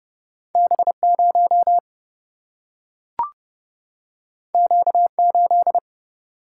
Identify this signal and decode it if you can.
Morse code, heard as a keyed tone, about 30 wpm, 700 Hz. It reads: B0 Q8
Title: Single-Letter Single-Number 30wpm